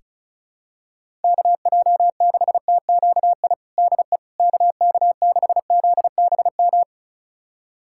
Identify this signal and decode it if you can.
Morse code, heard as a keyed tone, about 36 wpm, 700 Hz. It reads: KJ6TQI DE KK6ZBM
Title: USA Callsign DE Callsign 35wpm